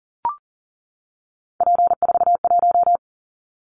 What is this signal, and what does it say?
Morse code, heard as a keyed tone, about 40 wpm, 700 Hz. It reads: P41
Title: Callsign Prefixes for Top 100 DXCC - 40wpm